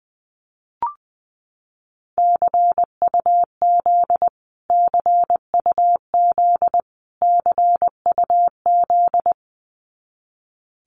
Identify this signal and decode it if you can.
Morse code, heard as a keyed tone, about 20 wpm, 700 Hz. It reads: CUZ CUZ CUZ
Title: Abbreviations Repeated 3x - 20wpm